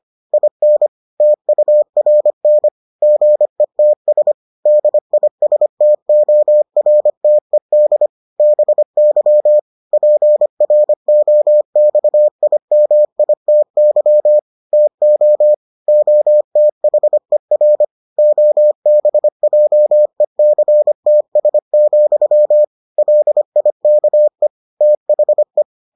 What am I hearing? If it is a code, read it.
Morse code, heard as a keyed tone, about 25 wpm, 600 Hz. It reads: IN TURN GETS DISTORTED BY PROXIMITY TO OTHER OBJECTS, LIKE THE